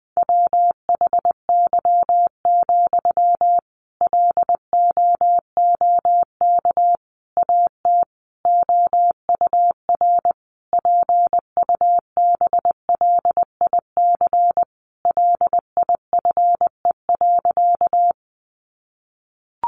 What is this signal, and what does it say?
Morse code, heard as a keyed tone, about 20 wpm, 700 Hz. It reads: WHY, LOOK AT OUR PUBLIC LIFE.
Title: Sentences from Top 600 Words 20wpm